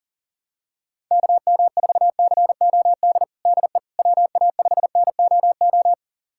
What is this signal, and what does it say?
Morse code, heard as a keyed tone, about 40 wpm, 690 Hz. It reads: KM4COD DE WA5NOO
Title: USA Callsign DE Callsign - 6 Characters - 40wpm